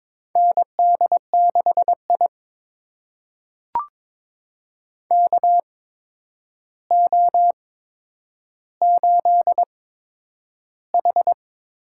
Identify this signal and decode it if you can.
Morse code, heard as a keyed tone, about 22 wpm, spaced as for 9 wpm, 700 Hz. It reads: ND6I KO8H
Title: USA Call Signs - 4 Characters - 8x Character Spacing - 22wpm